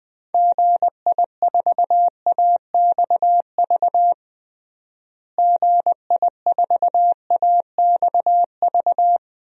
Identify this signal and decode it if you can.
Morse code, heard as a keyed tone, about 21 wpm, 700 Hz. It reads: GI4AXV GI4AXV
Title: DXCC-100 Call Signs - 6 Characters - Repeated 3x - 20wpm